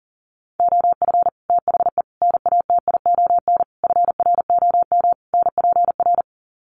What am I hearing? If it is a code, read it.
Morse code, heard as a keyed tone, about 40 wpm, 700 Hz. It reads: OF THE NATION FROM NPR
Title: Sets of 5 Words 40wpm